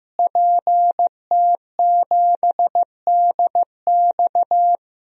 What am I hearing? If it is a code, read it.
Morse code, heard as a keyed tone, about 15 wpm, 700 Hz. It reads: PT7DX
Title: DXCC-100 Call Signs - 5 Characters - Repeated 3x - 15wpm